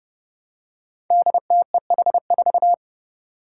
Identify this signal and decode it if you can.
Morse code, heard as a keyed tone, about 32 wpm, 700 Hz. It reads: DTEH4
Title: Five Challenging Letter-Number Combinations 30wpm